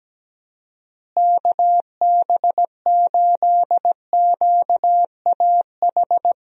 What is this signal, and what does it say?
Morse code, heard as a keyed tone, about 17 wpm, 700 Hz. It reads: KB8QAH
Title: USA Call Signs - 6 Characters - Repeated 2x - 17wpm